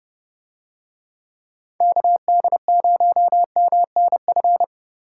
Morse code, heard as a keyed tone, about 29 wpm, 700 Hz. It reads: KD0MNF